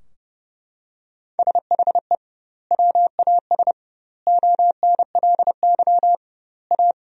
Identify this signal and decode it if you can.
Morse code, heard as a keyed tone, about 32 wpm, 700 Hz. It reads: SHE WAS ONLY A
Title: 2x Word-Spacing - Sentences from Top 400 Words - 30wpm